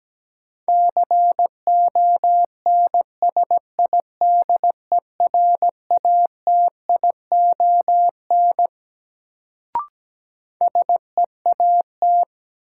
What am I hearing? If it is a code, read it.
Morse code, heard as a keyed tone, about 17 wpm, 700 Hz. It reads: CONSIDERATION SEAT
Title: Sentences from Top 1500 Words - New Word Drill Alt - 17wpm